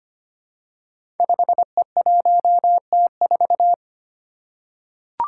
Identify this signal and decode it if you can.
Morse code, heard as a keyed tone, about 25 wpm, 700 Hz. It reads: 5E1T4
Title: Five Challenging Letter-Number Combinations 25wpm